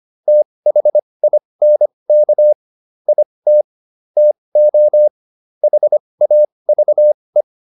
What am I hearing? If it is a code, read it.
Morse code, heard as a keyed tone, about 25 wpm, spaced as for 20 wpm, 600 Hz. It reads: THINK IT TO HAVE